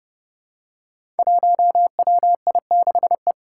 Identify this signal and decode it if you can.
Morse code, heard as a keyed tone, about 30 wpm, 700 Hz. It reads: 1WI6E
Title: Five Challenging Letter-Number Combinations 30wpm